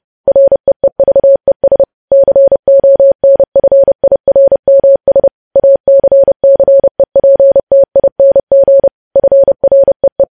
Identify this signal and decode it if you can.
Morse code, heard as a keyed tone, about 30 wpm, 550 Hz. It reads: REEVES CONFIRMS ACCEPTING FREE
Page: news headlines in morse code at 30 wpm